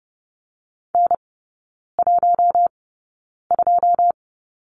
Morse code, heard as a keyed tone, about 30 wpm, spaced as for 9 wpm, 700 Hz. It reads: N12
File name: 081 Lesson 14e - Number 2 - Three Characters 30wpm